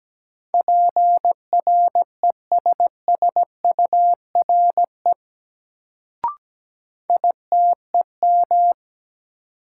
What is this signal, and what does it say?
Morse code, heard as a keyed tone, about 17 wpm, 700 Hz. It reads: PRESSURE ITEM
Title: Sentences from Top 1400 Words - New Word Drill Alt - 17wpm